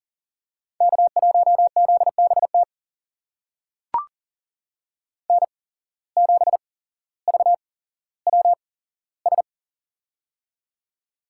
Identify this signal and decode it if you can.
Morse code, heard as a keyed tone, about 42 wpm, spaced as for 17 wpm, 700 Hz. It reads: K1ZBT N7VWS
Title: USA Call Signs - 5 Characters - 8x Character Spacing - 40wpm